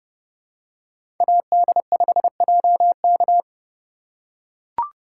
Morse code, heard as a keyed tone, about 28 wpm, 700 Hz. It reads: AD5JK